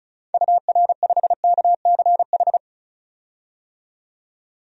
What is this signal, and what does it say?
Morse code, heard as a keyed tone, about 35 wpm, 700 Hz. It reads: UR5KCH
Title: DXCC-100 Call Signs - 6 Characters - 35wpm